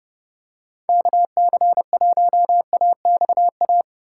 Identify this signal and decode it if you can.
Morse code, heard as a keyed tone, about 29 wpm, 700 Hz. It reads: KC1AXA